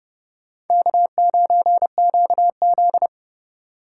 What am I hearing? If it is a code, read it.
Morse code, heard as a keyed tone, about 38 wpm, 700 Hz. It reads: K9QZ